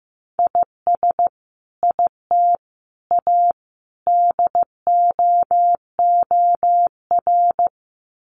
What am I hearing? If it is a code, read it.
Morse code, heard as a keyed tone, about 15 wpm, 700 Hz. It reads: IS IT A DOOR
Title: Sentences from Top 1300 Words 15wpm